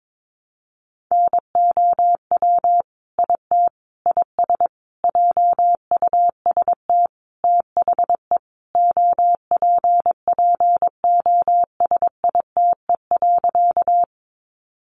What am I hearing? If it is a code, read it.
Morse code, heard as a keyed tone, about 22 wpm, 700 Hz. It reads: NOW IT IS JUST THE OPPOSITE.
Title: Sentences from Top 1300 Words 22wpm